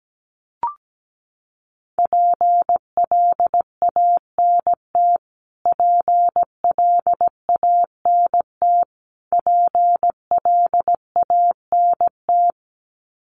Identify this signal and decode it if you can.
Morse code, heard as a keyed tone, about 17 wpm, 700 Hz. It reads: PLANT PLANT PLANT
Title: Top 300 Words Repeated 17wpm